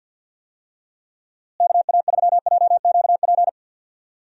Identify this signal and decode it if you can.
Morse code, heard as a keyed tone, about 50 wpm, 700 Hz. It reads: KA3JQP
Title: USA Call Signs - 6 Characters - Repeated 3x - 50wpm